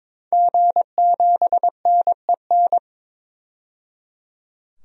Morse code, heard as a keyed tone, about 22 wpm, 700 Hz. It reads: G7NEN
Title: DXCC-100 Call Signs - 5 Characters - 22wpm